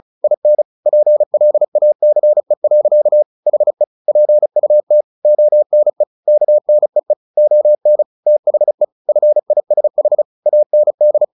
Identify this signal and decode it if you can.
Morse code, heard as a keyed tone, about 34 wpm, 600 Hz. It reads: IN PLACE. HE PUT ONE KNEE ON THE FISH AND
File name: the old man and the sea / the old man and the sea 1 35wpm